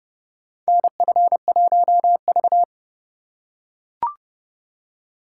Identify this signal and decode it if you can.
Morse code, heard as a keyed tone, about 30 wpm, 700 Hz. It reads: NF1V
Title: USA Call Signs - 4 Characters - 4x Character Spacing - 30wpm